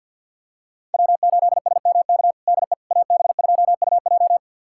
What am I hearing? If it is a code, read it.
Morse code, heard as a keyed tone, about 52 wpm, 700 Hz. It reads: W8SMK DE AB2UJ
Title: USA Callsign DE Callsign - 5 Characters - 50wpm